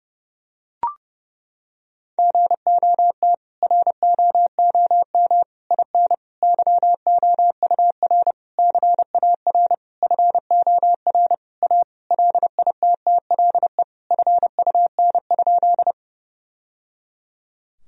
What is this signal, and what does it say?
Morse code, heard as a keyed tone, about 30 wpm, 700 Hz. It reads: GOT ROOM IN YOUR CAR FOR A LITTLE FUN?
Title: Sentences from Top 1000 Words - Review Part 1 - 30wpm